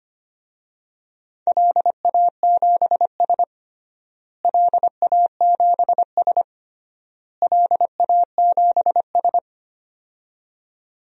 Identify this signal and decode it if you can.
Morse code, heard as a keyed tone, about 25 wpm, 700 Hz. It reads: LA7S LA7S LA7S
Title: DXCC-100 Call Signs - 4 Characters - Repeated 3x - 25wpm